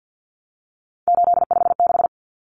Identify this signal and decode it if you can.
Morse code, heard as a keyed tone, about 50 wpm, 700 Hz. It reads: 856
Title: Numbers - 3-Digit 50wpm